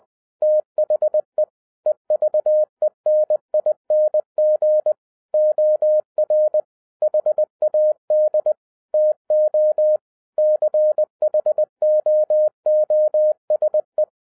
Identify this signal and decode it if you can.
Morse code, heard as a keyed tone, about 20 wpm, 610 Hz. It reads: THE EVENING OR HAD TO CHOOSE
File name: 2011-07-mp3 / n33 THE 3.11 2 20wpm